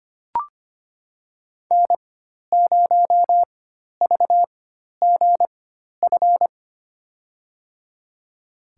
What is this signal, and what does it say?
Morse code, heard as a keyed tone, about 25 wpm, spaced as for 12 wpm, 700 Hz. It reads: N0VGF
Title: USA Call Signs - 4x Character Spacing - 25wpm